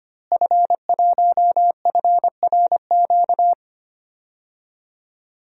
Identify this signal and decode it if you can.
Morse code, heard as a keyed tone, about 25 wpm, 700 Hz. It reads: F1FRQ
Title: DXCC-100 Call Signs - 5 Characters - Repeated 3x - 25wpm